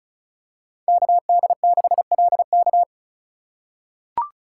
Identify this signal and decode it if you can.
Morse code, heard as a keyed tone, about 36 wpm, 700 Hz. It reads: KD6LK
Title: USA Call Signs - 5 Characters - 2x Character Spacing - 35wpm